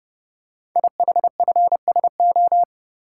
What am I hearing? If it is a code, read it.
Morse code, heard as a keyed tone, about 29 wpm, 700 Hz. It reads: IHFSO
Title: Code Groups - 5 Letters - 30wpm